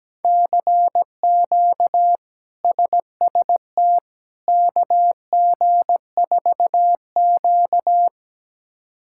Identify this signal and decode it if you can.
Morse code, heard as a keyed tone, about 17 wpm, 700 Hz. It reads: CQ SST KG4Q
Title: K1USN SST QSO Practice - 17wpm